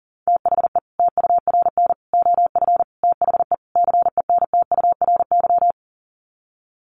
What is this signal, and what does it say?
Morse code, heard as a keyed tone, about 40 wpm, 700 Hz. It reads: THE TURN OF THE CENTURY
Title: Sets of 5 Words 40wpm